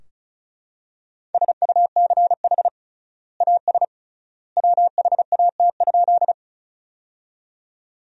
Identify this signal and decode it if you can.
Morse code, heard as a keyed tone, about 36 wpm, 700 Hz. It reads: SUCH AS WHAT?
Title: 3x Word-Spacing - Sentences from Top 200 Words - 35wpm